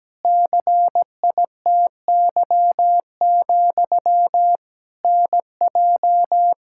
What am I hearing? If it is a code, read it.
Morse code, heard as a keyed tone, about 17 wpm, 700 Hz. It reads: CITY, NJ
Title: Practice QSO - Exchange 1 - Beginning - 17wpm